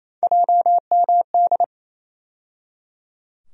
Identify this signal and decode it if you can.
Morse code, heard as a keyed tone, about 28 wpm, 700 Hz. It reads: JMD
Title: USA Call Sign Postfixes - 3 Letters 28wpm